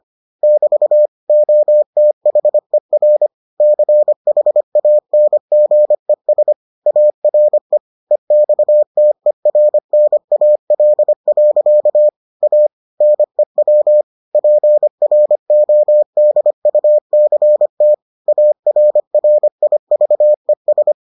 Morse code, heard as a keyed tone, about 25 wpm, 600 Hz. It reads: <BT> OTHER CHANGES ARE EXTERNAL. A NEW PRODUCT ARRIVES